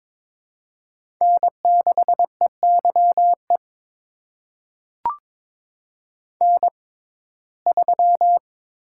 Morse code, heard as a keyed tone, about 22 wpm, spaced as for 16 wpm, 700 Hz. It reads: N6EYE N3
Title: USA Call Signs - 5 Characters - 6x Character Spacing - 22wpm